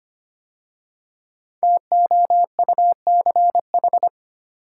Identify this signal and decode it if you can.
Morse code, heard as a keyed tone, about 25 wpm, 700 Hz. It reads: TOUCH
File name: 5 Letter Words - First 3 Letters Given - 25wpm